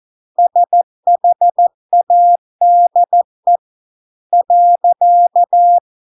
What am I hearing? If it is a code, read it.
Morse code, heard as a keyed tone, about 14 wpm, 700 Hz. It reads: SHADE .